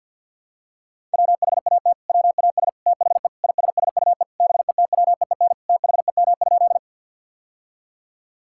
Morse code, heard as a keyed tone, about 50 wpm, 700 Hz. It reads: WHAT WAS THE ISSUE BETWEEN THEM?
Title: Sentences from Top 1000 Words - Review Part 6 - 50wpm